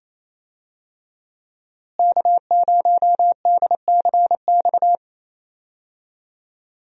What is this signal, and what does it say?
Morse code, heard as a keyed tone, about 28 wpm, 700 Hz. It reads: K0DCX
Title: USA Call Signs - 5 Characters - Repeated 2x - 28wpm